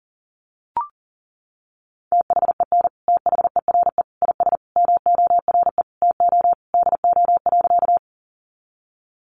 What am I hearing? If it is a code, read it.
Morse code, heard as a keyed tone, about 40 wpm, 700 Hz. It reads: THEN THERE IS MORE TO DO.
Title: Sentences from Top 100 Words 40wpm